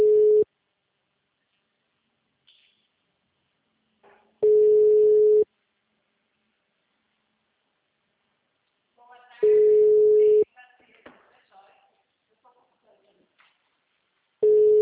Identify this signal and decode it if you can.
Ringback tone (Europe, 425 Hz), with speech.
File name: out- -2011- - - .364